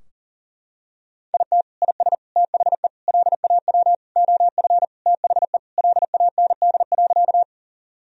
Morse code, heard as a keyed tone, about 40 wpm, 700 Hz. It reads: IT IS THE LAW OF THE LAND.
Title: Sentences from Top 1000 Words - Review Part 3 - 40wpm